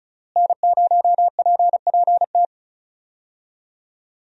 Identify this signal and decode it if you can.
Morse code, heard as a keyed tone, about 36 wpm, 700 Hz. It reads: N0PPT
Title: USA Call Signs - 5 Characters - Repeated 3x - 35wpm